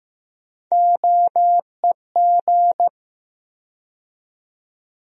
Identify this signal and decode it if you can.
Morse code, heard as a keyed tone, about 15 wpm, 700 Hz. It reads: OEG
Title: USA Call Sign Postfixes - 3 Letters 15wpm